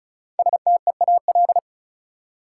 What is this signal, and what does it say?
Morse code, heard as a keyed tone, about 35 wpm, 700 Hz. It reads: STEAL
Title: Five Letter Words - 35wpm